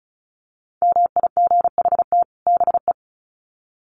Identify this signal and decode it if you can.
Morse code, heard as a keyed tone, about 34 wpm, 700 Hz. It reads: MIGHT BE
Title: Sets of 2 Words 35wpm